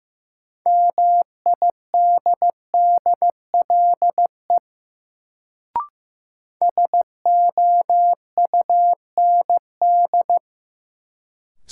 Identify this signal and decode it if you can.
Morse code, heard as a keyed tone, about 15 wpm, 700 Hz. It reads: MIDDLE SOUND
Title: Sentences from Top 900 Words - New Word Drill - 15wpm